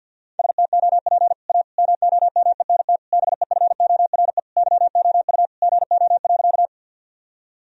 Morse code, heard as a keyed tone, about 46 wpm, 700 Hz. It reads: STOP A MOMENT BEFORE YOU GO.
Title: Sentences from Top 1000 Words - Review Part 1 - 50wpm